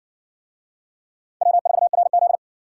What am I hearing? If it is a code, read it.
Morse code, heard as a keyed tone, about 60 wpm, 700 Hz. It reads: W4RZ